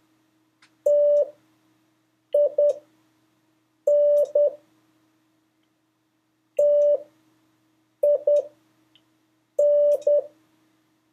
Morse code, heard as a keyed tone, about 10 wpm, spaced as for 6 wpm, 580 Hz. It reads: TIN TIN
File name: 2of5-words-1